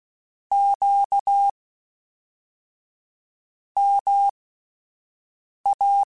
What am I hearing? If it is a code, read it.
Morse code, heard as a keyed tone, about 16 wpm, spaced as for 4 wpm, 800 Hz. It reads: QMA
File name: CW Comparisons / CW Comparisons-mp3s / CW Comparisons Broken Characters @10